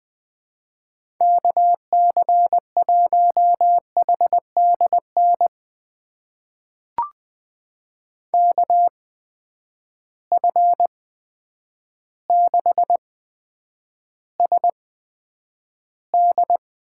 Morse code, heard as a keyed tone, about 20 wpm, spaced as for 9 wpm, 700 Hz. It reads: KC1HDN KF6SD
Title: USA Call Signs - 6 Characters - 8x Character Spacing - 20wpm